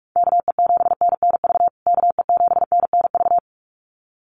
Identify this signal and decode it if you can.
Morse code, heard as a keyed tone, about 44 wpm, 700 Hz. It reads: KE7NNV KE7NNV
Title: USA Call Signs Repeated 2x 45wpm